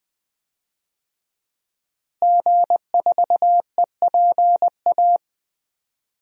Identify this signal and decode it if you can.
Morse code, heard as a keyed tone, about 20 wpm, 700 Hz. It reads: G4EPA
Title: DXCC-100 Call Signs - 5 Characters - Repeated 2x - 20wpm